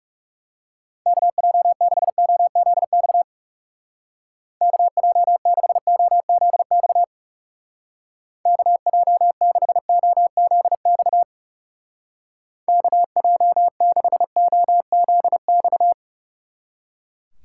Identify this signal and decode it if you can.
Morse code, heard as a keyed tone, about 37 wpm, 700 Hz. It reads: KJ6OZX KJ6OZX KJ6OZX KJ6OZX
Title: Speed Racing - USA 2x3 Call Signs 30wpm